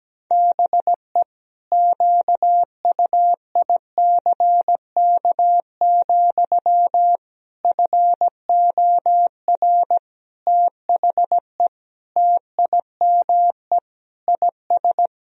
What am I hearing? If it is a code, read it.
Morse code, heard as a keyed tone, about 17 wpm, 700 Hz. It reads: BE QUICK, FOR THE TIME IS
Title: Sentences from Top 1000 Words - Review Part 2 - 17wpm